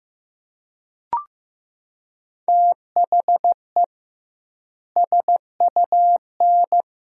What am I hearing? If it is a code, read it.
Morse code, heard as a keyed tone, about 15 wpm, 700 Hz. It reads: THE SUN
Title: 2x Word-Spacing - Sentences from Top 300 Words - 15wpm